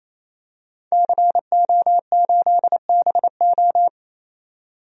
Morse code, heard as a keyed tone, about 28 wpm, 700 Hz. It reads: CO8BO
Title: DXCC-100 Call Signs - 5 Characters - 28wpm